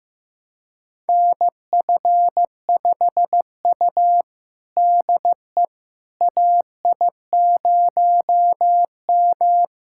Morse code, heard as a keyed tone, about 15 wpm, 700 Hz. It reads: NF5U DE AI0M
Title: USA Callsign DE Callsign - 4 Characters - 15wpm